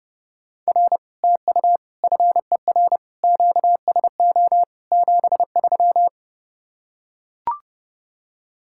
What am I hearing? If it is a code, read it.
Morse code, heard as a keyed tone, about 30 wpm, 700 Hz. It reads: R TU FER QSO 73